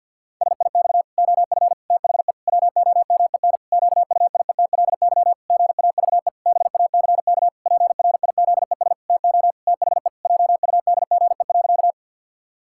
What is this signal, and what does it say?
Morse code, heard as a keyed tone, about 53 wpm, 700 Hz. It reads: SIX OF THE WOMEN QUIETLY GAVE BACK PRIZES TO THE JUDGE.
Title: Pangram Challenge v2 - 50wpm